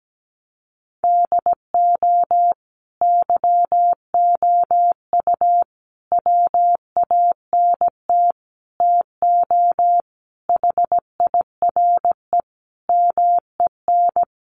Morse code, heard as a keyed tone, about 17 wpm, 700 Hz. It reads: DO YOU WANT TO HIRE MEN
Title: Sentences from Top 1400 Words 17wpm